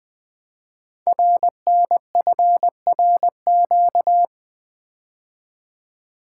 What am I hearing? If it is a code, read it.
Morse code, heard as a keyed tone, about 20 wpm, 700 Hz. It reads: RNFRQ
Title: Code Groups - 5 Letters - 20wpm